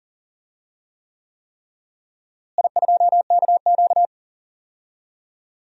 Morse code, heard as a keyed tone, about 40 wpm, 700 Hz. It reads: I2KQ
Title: DXCC-100 Call Signs - 4 Characters - Repeated 3x - 40wpm